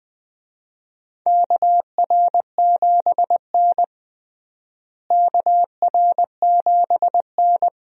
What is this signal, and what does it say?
Morse code, heard as a keyed tone, about 21 wpm, 700 Hz. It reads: KR7N KR7N
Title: USA Call Signs - 4 Characters - Repeated 2x - 20wpm